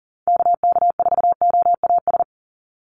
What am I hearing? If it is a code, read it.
Morse code, heard as a keyed tone, about 39 wpm, 700 Hz. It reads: KK4OAS
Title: USA Call Signs 40wpm